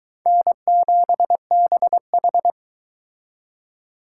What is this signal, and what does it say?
Morse code, heard as a keyed tone, about 22 wpm, 700 Hz. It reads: N7BH